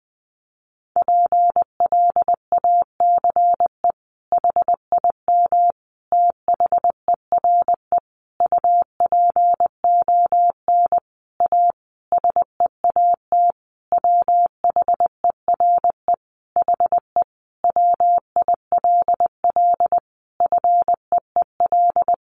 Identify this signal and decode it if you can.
Morse code, heard as a keyed tone, about 20 wpm, 700 Hz. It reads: PLACE HIM THERE UPON A SEAT WHERE HE WILL FEEL
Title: Sentences from Top 1500 Words 20wpm